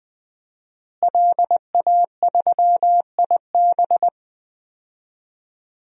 Morse code, heard as a keyed tone, about 20 wpm, 700 Hz. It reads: LA3IB
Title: DXCC-100 Call Signs - 5 Characters - Repeated 3x - 20wpm